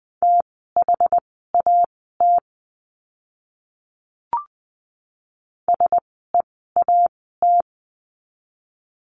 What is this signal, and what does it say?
Morse code, heard as a keyed tone, about 20 wpm, spaced as for 14 wpm, 700 Hz. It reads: THAT SEAT
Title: 061 Lesson 11f - Letter h - Words 20wpm